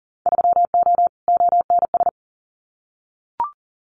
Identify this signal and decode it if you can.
Morse code, heard as a keyed tone, about 38 wpm, 700 Hz. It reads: 3O ONS
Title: Winter Field Day Exchanges Encore 40wpm